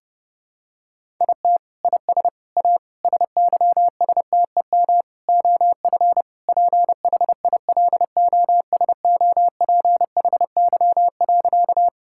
Morse code, heard as a keyed tone, about 30 wpm, 700 Hz. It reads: IT IS A SYSTEM OF PHILOSOPHY.
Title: Sentences from Top 1000 Words - Review Part 6 - 30wpm